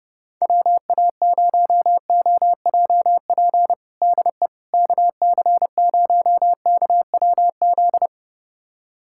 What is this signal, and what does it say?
Morse code, heard as a keyed tone, about 31 wpm, 700 Hz. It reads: WA0OJP DE KC0KWZ
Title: USA Callsign DE Callsign - 6 Characters - 30wpm